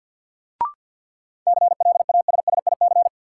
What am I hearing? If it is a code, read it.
Morse code, heard as a keyed tone, about 53 wpm, 700 Hz. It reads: CLASSIC
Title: Sentences from Top 1000 Words - New Word Drill - 50wpm